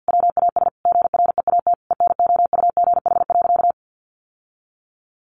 Morse code, heard as a keyed tone, about 50 wpm, 700 Hz. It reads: WAS GREAT ENOUGH.
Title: Sentences from Top 1500 Words 50wpm